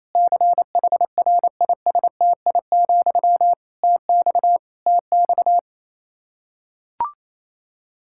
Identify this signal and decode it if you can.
Morse code, heard as a keyed tone, about 28 wpm, 700 Hz. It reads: CHRISTI, TX TX
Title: K1USN SST QSO Practice 28wpm